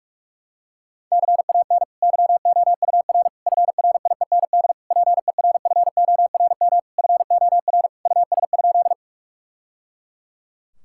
Morse code, heard as a keyed tone, about 42 wpm, 700 Hz. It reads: CAN YOUR FRIEND PERFORM FOR US?
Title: Sentences from Top 1000 Words - Review Part 5 - 45wpm